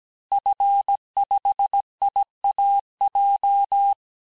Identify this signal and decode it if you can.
Morse code, heard as a keyed tone, about 17 wpm, 800 Hz. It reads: F5IAJ
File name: corrige-cours-1581-fin-du-QSO-vitesse-16-mpm